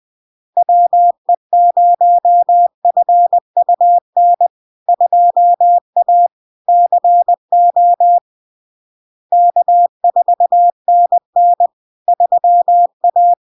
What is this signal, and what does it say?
Morse code, heard as a keyed tone, about 20 wpm, 700 Hz. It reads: WE0FUN 2A CO K4NN 3A
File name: LARK-FD-2022-Log-in-CW-20wpm-700-Hz